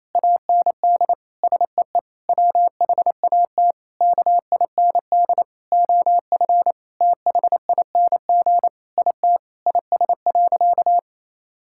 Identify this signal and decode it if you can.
Morse code, heard as a keyed tone, about 28 wpm, 700 Hz. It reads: AND SEE WHAT KIND OF THING IT IS.
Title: Sentences from Top 1000 Words - Review Part 6 - 28wpm